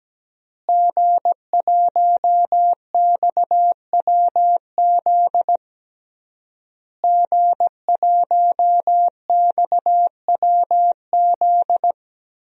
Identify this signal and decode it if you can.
Morse code, heard as a keyed tone, about 17 wpm, 700 Hz. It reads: G1XWZ G1XWZ
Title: DXCC-100 Call Signs - 5 Characters - Repeated 3x - 17wpm